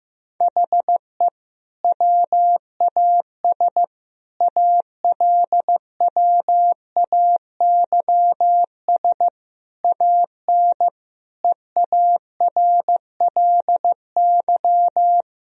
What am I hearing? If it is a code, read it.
Morse code, heard as a keyed tone, about 15 wpm, 700 Hz. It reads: HE WAS ALWAYS AN EARLY
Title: Sentences from Top 1000 Words - Review Part 2 - 15wpm